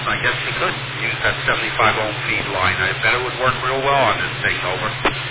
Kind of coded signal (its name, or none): Emergency Alert System (SAME) tones